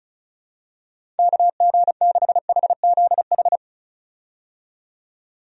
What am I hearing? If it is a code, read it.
Morse code, heard as a keyed tone, about 36 wpm, 700 Hz. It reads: KG6HZH
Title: USA Call Signs - 6 Characters - 35wpm